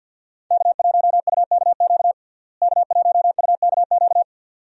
Morse code, heard as a keyed tone, about 49 wpm, 700 Hz. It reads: K1UKQ K1UKQ
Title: USA Call Signs - 5 Characters - Repeated 2x - 50wpm